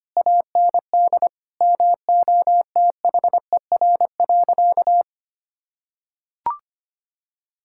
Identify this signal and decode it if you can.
Morse code, heard as a keyed tone, about 25 wpm, 700 Hz. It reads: AND MOTHER.
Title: Sentences from Top 1600 Words 25wpm